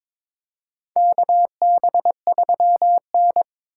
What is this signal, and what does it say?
Morse code, heard as a keyed tone, about 22 wpm, 700 Hz. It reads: KB3N
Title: USA Call Signs - 4 Characters - 3x Character Spacing - 22wpm